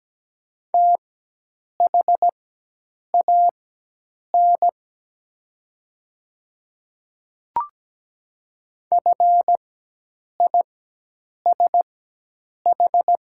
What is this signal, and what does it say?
Morse code, heard as a keyed tone, about 17 wpm, spaced as for 8 wpm, 700 Hz. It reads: THAN FISH
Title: 4x Character Spacing - Top 300 Words - 17wpm